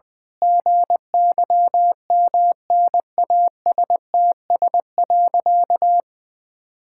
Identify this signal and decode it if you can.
Morse code, heard as a keyed tone, about 21 wpm, 700 Hz. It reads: GYMNASTS.
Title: Pangram Challenge v2 - 20wpm